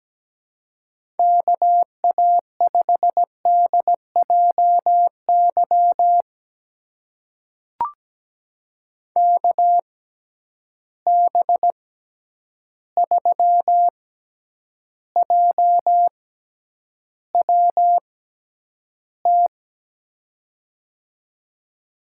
Morse code, heard as a keyed tone, about 17 wpm, spaced as for 9 wpm, 700 Hz. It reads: KA5DJY KB3JWT
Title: USA Call Signs - 6 Characters - 6x Character Spacing - 17wpm